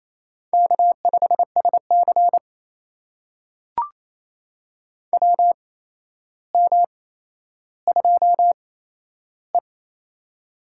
Morse code, heard as a keyed tone, about 28 wpm, spaced as for 12 wpm, 700 Hz. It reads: K5SC WM2E
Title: USA Call Signs - 4 Characters - 8x Character Spacing - 28wpm